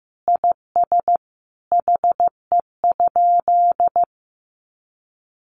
Morse code, heard as a keyed tone, about 15 wpm, 700 Hz. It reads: IS HE?
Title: Sentences from Top 1300 Words 15wpm